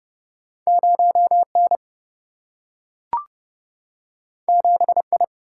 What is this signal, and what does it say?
Morse code, heard as a keyed tone, about 30 wpm, 700 Hz. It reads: 0N 7I
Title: Single-Letter Single-Number 30wpm